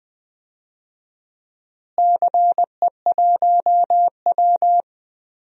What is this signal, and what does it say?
Morse code, heard as a keyed tone, about 20 wpm, 700 Hz. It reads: CE1W
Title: DXCC-100 Call Signs - 4 Characters - Repeated 2x - 20wpm